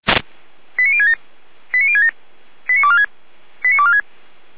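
Five-tone selective call (ZVEI), heard as quick runs of five tones, tones of about 70 ms. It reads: 89067 89067 89246 89246
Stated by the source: ZVEI five-tone selective call received via SDR.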